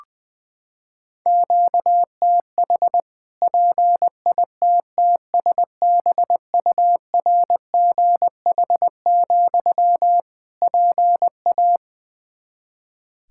Morse code, heard as a keyed tone, about 20 wpm, 700 Hz. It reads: QTH PITTSBURGH, PA
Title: Practice QSO - Exchange 1 - Beginning - 20wpm